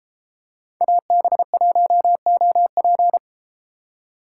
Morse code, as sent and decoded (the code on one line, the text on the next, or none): .- -... .---- --- .--.
AB1OP